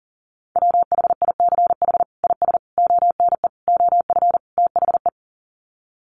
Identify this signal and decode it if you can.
Morse code, heard as a keyed tone, about 40 wpm, 700 Hz. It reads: WHICH IS ONE OF THE
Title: Sets of 5 Words Encore 40wpm